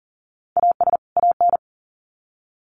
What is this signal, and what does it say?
Morse code, heard as a keyed tone, about 40 wpm, 700 Hz. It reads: AS AN
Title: Sets of 2 Words 40wpm